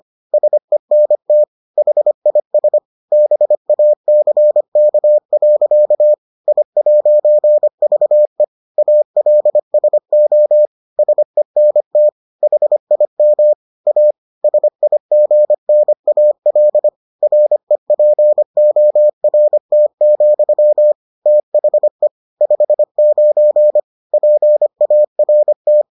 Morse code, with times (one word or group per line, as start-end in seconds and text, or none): 0.3-1.4 SENT
1.8-2.8 HIS
3.1-6.1 BACK.
6.5-8.4 I'VE
8.8-10.7 ALSO
11.0-12.1 SENT
12.4-13.5 HIM
13.9-14.1 A
14.4-16.9 SIGNAL
17.2-20.9 REPORT,
21.3-22.1 THE
22.4-23.8 59
24.1-25.8 PART